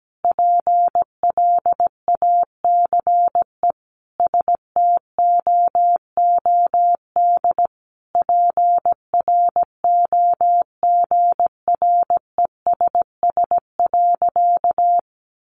Morse code, heard as a keyed tone, about 17 wpm, 700 Hz. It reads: PLACE STOOD PROGRESS.
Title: Sentences from Top 1400 Words 17wpm